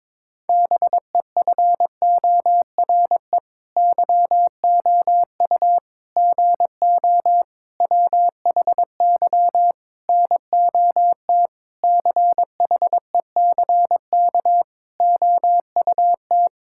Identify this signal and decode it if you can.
Morse code, heard as a keyed tone, about 22 wpm, 700 Hz. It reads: BEFORE YOU GO WHY NOT CHECK OUT
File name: Sets of 9 Words - 22wpm